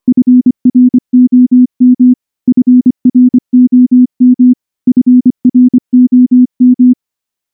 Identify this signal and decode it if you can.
Morse code, heard as a keyed tone, about 23 wpm, 250 Hz. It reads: FROM FROM FROM
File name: morse / morse-training-25WPM